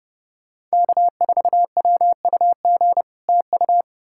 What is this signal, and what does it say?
Morse code, heard as a keyed tone, about 37 wpm, 700 Hz. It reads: K4WUG TU
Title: Speed-Racing Common QSO Phrases - 30wpm